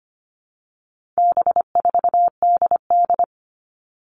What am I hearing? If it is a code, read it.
Morse code, heard as a keyed tone, about 25 wpm, 700 Hz. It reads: B4DD
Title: All Cut Numbers - 4-Digit 25wpm